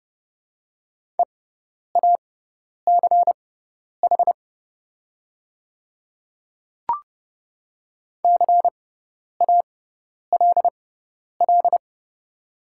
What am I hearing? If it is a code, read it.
Morse code, heard as a keyed tone, about 30 wpm, spaced as for 10 wpm, 690 Hz. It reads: EACH CALL
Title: 6x Character Spacing - Top 100 Words - 30wpm